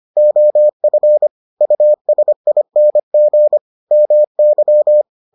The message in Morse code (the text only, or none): OF USING MY